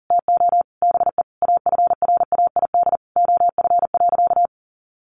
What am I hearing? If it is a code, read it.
Morse code, heard as a keyed tone, about 37 wpm, 700 Hz. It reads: TO BE AFRAID OF.
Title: Sentences from Top 1500 Words 40wpm